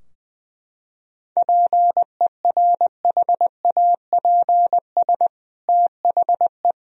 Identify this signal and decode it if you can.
Morse code, heard as a keyed tone, about 20 wpm, 700 Hz. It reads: PERHAPS THE
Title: Sentences from Top 1000 Words - Review Part 4 - 20wpm